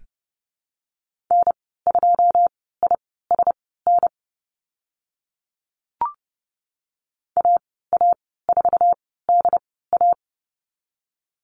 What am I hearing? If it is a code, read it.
Morse code, heard as a keyed tone, about 30 wpm, spaced as for 17 wpm, 700 Hz. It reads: N2ISN AA4DA
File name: Lesson 14g - Number 2 - Call Signs 30wpm